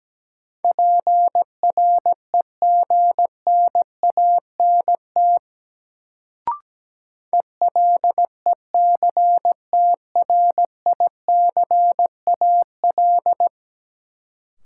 Morse code, heard as a keyed tone, about 17 wpm, 700 Hz. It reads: PREGNANT ELECTRICAL
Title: Sentences from Top 1000 Words - Review Part 6 - New Word Drill - 17wpm